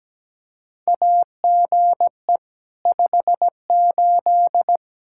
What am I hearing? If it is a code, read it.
Morse code, heard as a keyed tone, about 17 wpm, 700 Hz. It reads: AGE 58
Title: Practice QSO - Exchange 3 - More Info - 17wpm